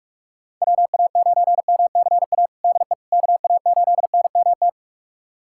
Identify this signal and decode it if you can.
Morse code, heard as a keyed tone, about 41 wpm, 700 Hz. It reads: WA9MCA DE KA8NMT
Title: USA Callsign DE Callsign - 6 Characters - 45wpm